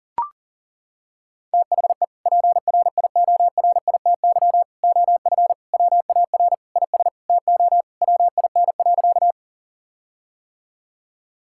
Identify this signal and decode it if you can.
Morse code, heard as a keyed tone, about 37 wpm, 700 Hz. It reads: THE PRIORITY OF WAR IS TO WIN.
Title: Sentences from Top 1600 Words 40wpm